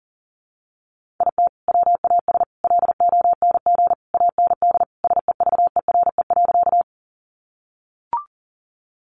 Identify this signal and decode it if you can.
Morse code, heard as a keyed tone, about 40 wpm, 700 Hz. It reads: IT WAS LONG AND SEVERE.
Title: Sentences from Top 1400 Words 40wpm